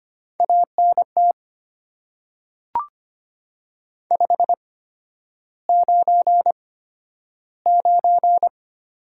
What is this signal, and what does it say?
Morse code, heard as a keyed tone, about 25 wpm, spaced as for 11 wpm, 700 Hz. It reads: ANT 599
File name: Top 90 QSO Elements - 8x Character Spacing - 25wpm